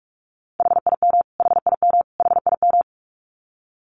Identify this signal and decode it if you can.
Morse code, heard as a keyed tone, about 44 wpm, 700 Hz. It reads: HIM HIM HIM
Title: Top 200 Words Repeated 45wpm